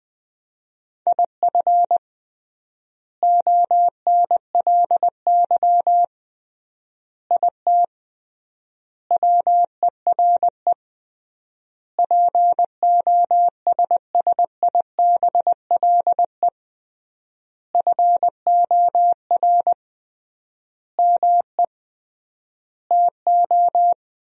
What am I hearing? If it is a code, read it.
Morse code, heard as a keyed tone, about 20 wpm, 700 Hz. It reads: IF ONLY IT WERE POSSIBLE FOR ME TO
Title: 3x Word-Spacing - Sentences from Top 500 Words - 20wpm